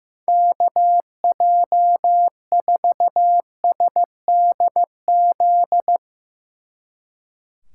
Morse code, heard as a keyed tone, about 15 wpm, 700 Hz. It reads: KJ4SDZ
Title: Speed Racing - USA 2x3 Call Signs 15wpm